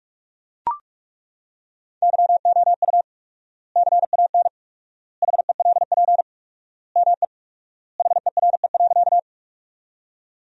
Morse code, heard as a keyed tone, about 42 wpm, 700 Hz. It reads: YOU CAN HELP ME HERE.
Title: 4x Word-Spacing - Sentences from Top 200 Words - 45wpm